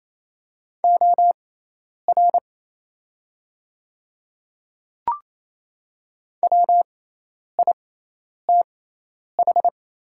Morse code, heard as a keyed tone, about 27 wpm, spaced as for 10 wpm, 690 Hz. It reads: OR WITH
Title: 6x Character Spacing - Top 100 Words - 28wpm